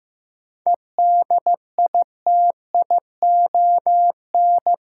Morse code, heard as a keyed tone, about 16 wpm, 700 Hz. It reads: EDITION
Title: Sets of 5 Words - New Word Drill - 15wpm